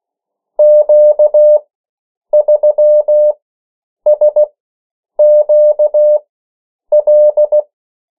Morse code, heard as a keyed tone, about 16 wpm, spaced as for 8 wpm, 600 Hz. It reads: Q3SQL
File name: morse mp3 / 14 urok 13